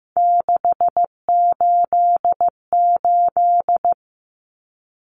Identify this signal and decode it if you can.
Morse code, heard as a keyed tone, about 15 wpm, 700 Hz. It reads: 688
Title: Numbers - 3-Digit 15wpm